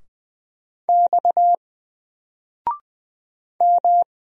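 Morse code, heard as a keyed tone, about 20 wpm, 700 Hz. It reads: X M
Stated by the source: Single letter and number Morse drill at 20 wpm, from N4LCD.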